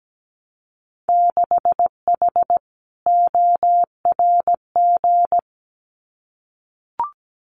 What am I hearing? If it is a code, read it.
Morse code, heard as a keyed tone, about 17 wpm, 700 Hz. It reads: 6H ORG
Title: Winter Field Day Exchanges Encore 17wpm